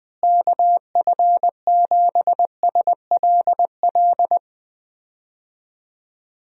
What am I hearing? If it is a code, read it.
Morse code, heard as a keyed tone, about 20 wpm, 700 Hz. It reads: KF7SLL
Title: USA Call Signs - 6 Characters - Repeated 2x - 20wpm